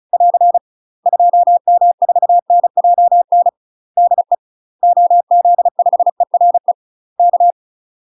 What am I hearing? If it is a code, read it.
Morse code, heard as a keyed tone, about 34 wpm, 700 Hz. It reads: <AR> 2M4NJN DE OZ5ERE K
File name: UK MASTER QSOs TRAINING / 35WPM / UK MASTER QSOs 0006